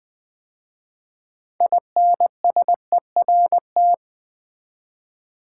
Morse code, heard as a keyed tone, about 20 wpm, 700 Hz. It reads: INSERT
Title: 6 Letter Words - First 3 Letters Given - 20wpm